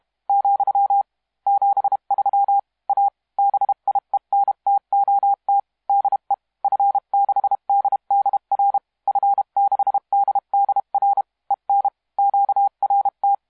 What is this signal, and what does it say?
Morse code, heard as a keyed tone, about 32 wpm, 800 Hz. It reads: , 73 A BIENTOT DE F6DDR F6DDR EN QRT
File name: musiqueradio / cw hsc demo f6ddr